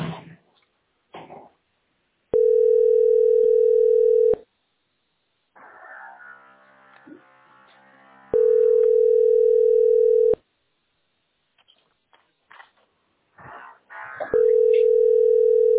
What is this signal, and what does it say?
Ringback tone (North America), 440 + 480 Hz.